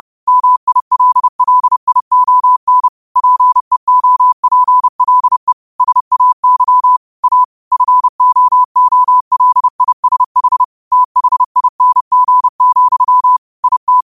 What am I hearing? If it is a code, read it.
Morse code, heard as a keyed tone, about 30 wpm, 1000 Hz. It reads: MILLION PEOPLE SAY A FOOLISH THING, IT